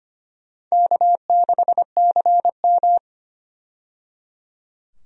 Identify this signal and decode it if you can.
Morse code, heard as a keyed tone, about 25 wpm, 700 Hz. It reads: K6CM
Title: USA Call Signs - 4 Characters - 25wpm